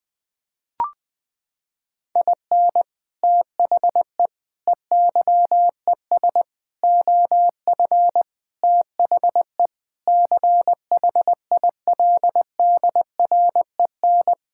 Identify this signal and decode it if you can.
Morse code, heard as a keyed tone, about 19 wpm, 700 Hz. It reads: IN THE EYES OF THE CHILDREN
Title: Sentences with 5-Word Phrases - 20wpm